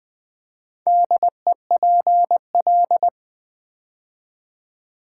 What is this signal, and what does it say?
Morse code, heard as a keyed tone, about 20 wpm, 700 Hz. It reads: DEPL
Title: Build A Word - 2 Letters At A Time - 6-Letter Words - 20wpm